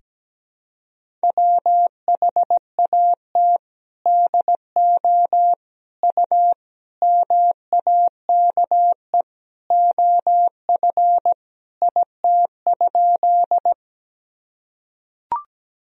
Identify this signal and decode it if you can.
Morse code, heard as a keyed tone, about 17 wpm, 700 Hz. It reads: WHAT DO U MAKE OF IT?
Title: Sentences from Top 100 Words and Abbreviations 17wpm